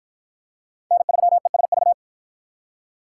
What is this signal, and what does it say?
Morse code, heard as a keyed tone, about 50 wpm, 700 Hz. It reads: N3ESV